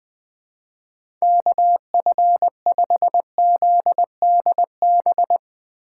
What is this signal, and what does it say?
Morse code, heard as a keyed tone, about 20 wpm, 700 Hz. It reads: KF5ZDB